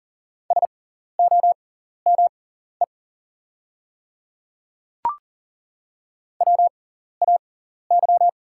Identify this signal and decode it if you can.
Morse code, heard as a keyed tone, about 39 wpm, spaced as for 14 wpm, 690 Hz. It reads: SOME WAY
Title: 6x Character Spacing - Top 100 Words - 40wpm